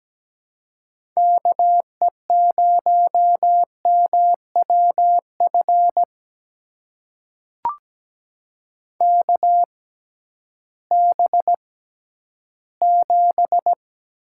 Morse code, heard as a keyed tone, about 17 wpm, spaced as for 11 wpm, 700 Hz. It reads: KE0MWF KB7
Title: USA Call Signs - 6x Character Spacing - 17wpm